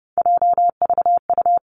Morse code, heard as a keyed tone, about 31 wpm, 700 Hz. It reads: JVU
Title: Three Challenging Letter-Number Combinations 30wpm